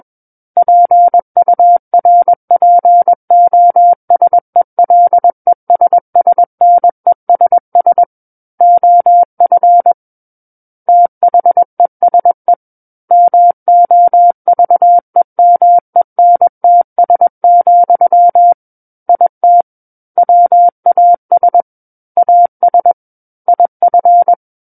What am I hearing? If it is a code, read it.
Morse code, heard as a keyed tone, about 21 wpm, 700 Hz. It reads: PURPOSELESSNESS OF THESE MOVEMENTS, IT WAS AS IF
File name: Book One / 12-Chapter 12 - 21 WPM